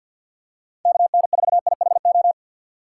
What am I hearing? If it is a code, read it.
Morse code, heard as a keyed tone, about 53 wpm, 700 Hz. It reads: KN4IHO
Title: USA Call Signs - 6 Characters - 4x Character Spacing - 50wpm